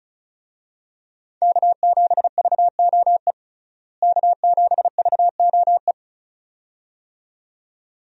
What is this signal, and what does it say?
Morse code, heard as a keyed tone, about 34 wpm, 700 Hz. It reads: K7VOE K7VOE
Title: USA Call Signs - 5 Characters - Repeated 2x - 35wpm